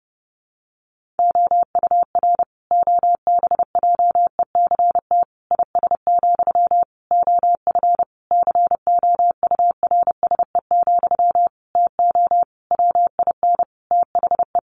Morse code, heard as a keyed tone, about 30 wpm, 700 Hz. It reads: OUR OBJECT IS, OF COURSE, TO WIN THE
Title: Sentences from Top 1500 Words 30wpm